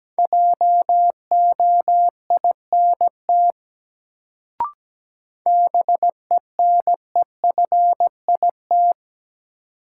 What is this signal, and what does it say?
Morse code, heard as a keyed tone, about 17 wpm, 700 Hz. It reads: JOINT BENEFIT
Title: Sets of 5 Words Encore - New Word Drill - 17wpm